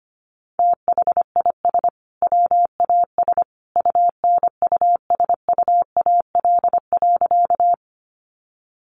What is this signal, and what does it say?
Morse code, heard as a keyed tone, about 25 wpm, 700 Hz. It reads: THIS WAS UNUSUAL.
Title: Sentences from Top 1400 Words 25wpm